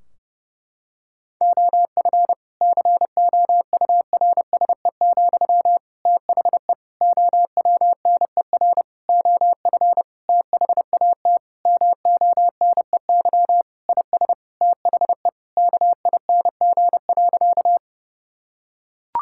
Morse code, heard as a keyed tone, about 30 wpm, 700 Hz. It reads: OF COURSE, THE OWNER OF THAT MONEY IS THE KING.
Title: Sentences from Top 1600 Words 30wpm